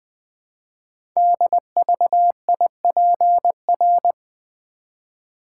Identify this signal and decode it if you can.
Morse code, heard as a keyed tone, about 20 wpm, 700 Hz. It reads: DVIPR
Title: Code Groups - 5 Letters - 20wpm